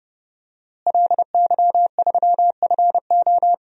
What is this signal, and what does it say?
Morse code, heard as a keyed tone, about 31 wpm, 700 Hz. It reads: LY3FO
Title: DXCC-100 Call Signs - 5 Characters - Repeated 2x - 30wpm